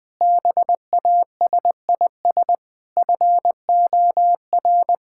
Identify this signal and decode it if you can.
Morse code, heard as a keyed tone, about 20 wpm, 700 Hz. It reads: BASIS FOR
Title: Sentences from Top 1000 Words - Review Part 5 - 20wpm